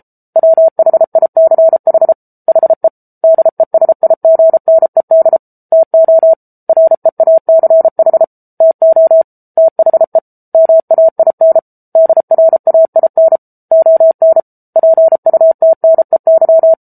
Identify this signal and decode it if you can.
Morse code, heard as a keyed tone, about 32 wpm, 650 Hz. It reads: WHICH HE DESIGNED TO REACH TO THE MAIN DRAIN ON PUTNEY
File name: Book Two / 07-Chapter 7 - 33 WPM